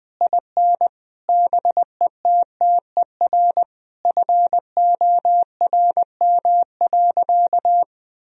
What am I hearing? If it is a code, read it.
Morse code, heard as a keyed tone, about 20 wpm, 700 Hz. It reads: IN BETTER FORM.
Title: Sentences from Top 1000 Words - Review Part 1 - 20wpm